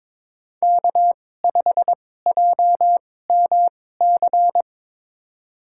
Morse code, heard as a keyed tone, about 22 wpm, spaced as for 16 wpm, 700 Hz. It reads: K5JMC
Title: USA Call Signs - 5 Characters - 2x Character Spacing - 22wpm